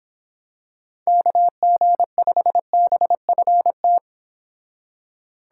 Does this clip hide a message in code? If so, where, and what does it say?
Yes: Morse code from 1.1 to 4.0 s, heard as a keyed tone, about 26 wpm, 700 Hz. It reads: KG5BFT